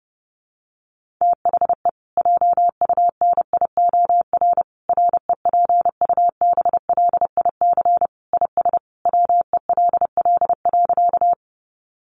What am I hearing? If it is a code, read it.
Morse code, heard as a keyed tone, about 30 wpm, 700 Hz. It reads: THE JUNIOR REPUBLIC IS WELL.
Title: Sentences from Top 1500 Words 30wpm